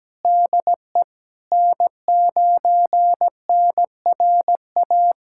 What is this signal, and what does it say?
Morse code, heard as a keyed tone, about 17 wpm, 700 Hz. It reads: DE N9NRA
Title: USA Callsign DE Callsign - 5 Characters - 17wpm